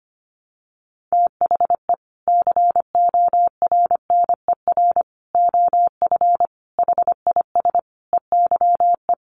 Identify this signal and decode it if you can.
Morse code, heard as a keyed tone, about 25 wpm, 700 Hz. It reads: THE CORNER OF HIS EYE
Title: Sets of 5 Words Encore2 25wpm